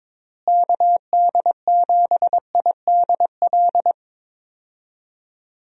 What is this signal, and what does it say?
Morse code, heard as a keyed tone, about 22 wpm, 700 Hz. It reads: KD7IDL
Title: USA Call Signs - 6 Characters - Repeated 2x - 22wpm